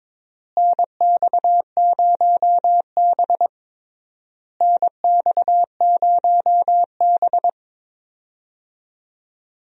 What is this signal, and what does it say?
Morse code, heard as a keyed tone, about 22 wpm, 700 Hz. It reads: NX0B NX0B
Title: USA Call Signs - 4 Characters - Repeated 3x - 22wpm